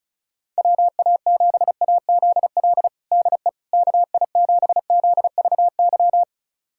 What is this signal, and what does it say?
Morse code, heard as a keyed tone, about 34 wpm, 700 Hz. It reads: WA7AZL DE KI7ZVY
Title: USA Callsign DE Callsign - 6 Characters - 35wpm